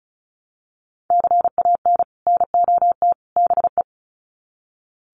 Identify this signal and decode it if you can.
Morse code, heard as a keyed tone, about 34 wpm, 700 Hz. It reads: CAN NOT BE
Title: Sets of 3 Words 35wpm